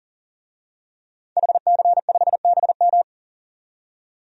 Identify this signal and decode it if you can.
Morse code, heard as a keyed tone, about 40 wpm, 700 Hz. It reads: HC5BM
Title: DXCC-100 Call Signs - 5 Characters - 40wpm